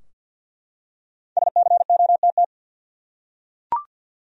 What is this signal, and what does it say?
Morse code, heard as a keyed tone, about 47 wpm, 700 Hz. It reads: SCOTT
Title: USA Common Names - Max 5 Letters - 50wpm